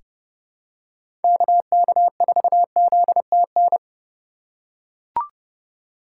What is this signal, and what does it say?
Morse code, heard as a keyed tone, about 29 wpm, 700 Hz. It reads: KK4ZTN
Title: USA Call Signs - 6 Characters - 4x Character Spacing - 30wpm